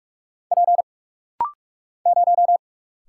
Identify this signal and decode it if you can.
Morse code, heard as a keyed tone, about 44 wpm, 700 Hz. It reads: P 0
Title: Single Letter-Number Alt - Rapid-Fire 45wpm